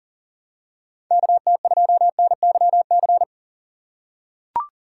Morse code, heard as a keyed tone, about 40 wpm, 700 Hz. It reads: KT2NYC
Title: USA Call Signs - 6 Characters - 3x Character Spacing - 40wpm